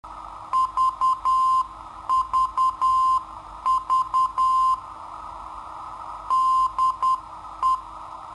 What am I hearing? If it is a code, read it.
Morse code, heard as a keyed tone, about 10 wpm, 1040 Hz. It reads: VVV DE
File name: sitoariold / beacon hk